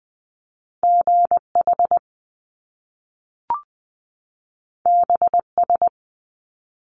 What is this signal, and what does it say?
Morse code, heard as a keyed tone, about 20 wpm, 700 Hz. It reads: GH BS
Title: Top Two Letter Combinations in Words 20wpm